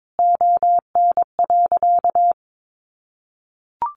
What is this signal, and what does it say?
Morse code, heard as a keyed tone, about 22 wpm, 700 Hz. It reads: ON.
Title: Easy Phrases 22wpm